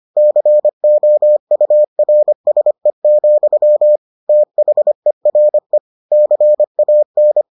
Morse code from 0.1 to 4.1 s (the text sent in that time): COURSE,